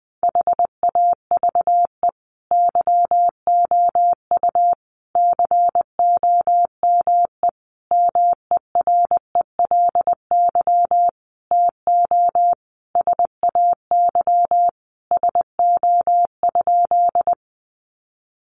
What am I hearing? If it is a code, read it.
Morse code, heard as a keyed tone, about 19 wpm, 700 Hz. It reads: HAVE YOU COME MERELY TO SAY SO?
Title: Sentences from Top 1400 Words 20wpm